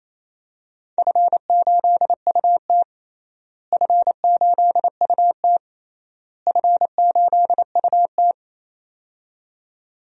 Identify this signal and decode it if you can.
Morse code, heard as a keyed tone, about 28 wpm, 700 Hz. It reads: F8UT F8UT F8UT
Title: DXCC-100 Call Signs - 4 Characters - Repeated 3x - 28wpm